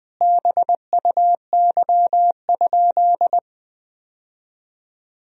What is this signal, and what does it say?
Morse code, heard as a keyed tone, about 20 wpm, 700 Hz. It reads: BUY?
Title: Sentences from Top 1000 Words - Review Part 4 - 20wpm